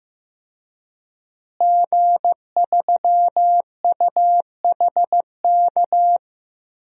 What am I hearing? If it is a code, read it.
Morse code, heard as a keyed tone, about 15 wpm, 700 Hz. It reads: G3UHK
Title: DXCC-100 Call Signs - 5 Characters - Repeated 3x - 15wpm